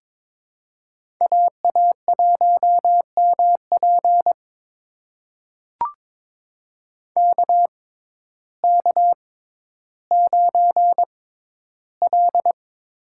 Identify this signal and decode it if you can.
Morse code, heard as a keyed tone, about 23 wpm, spaced as for 12 wpm, 700 Hz. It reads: AA1MP KK9L
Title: USA Call Signs - 6x Character Spacing - 22wpm